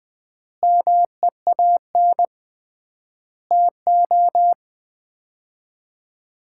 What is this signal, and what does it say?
Morse code, heard as a keyed tone, about 20 wpm, 700 Hz. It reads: MEAN TO
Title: 3x Word-Spacing - Sets of 5 Words - 20wpm